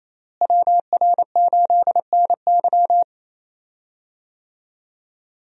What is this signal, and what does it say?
Morse code, heard as a keyed tone, about 28 wpm, 700 Hz. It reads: WR8NY
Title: USA Call Signs - 5 Characters - Repeated 2x - 28wpm